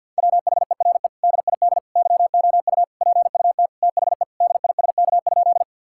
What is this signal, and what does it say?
Morse code, heard as a keyed tone, about 50 wpm, 700 Hz. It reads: WHERE DID YOU PUT THE DISK?
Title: Sentences from Top 1000 Words - Review Part 6 - 50wpm